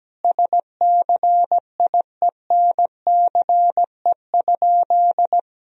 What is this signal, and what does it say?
Morse code, heard as a keyed tone, about 17 wpm, 700 Hz. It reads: SCIENCE?
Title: Sentences from Top 1000 Words - Review Part 2 - 17wpm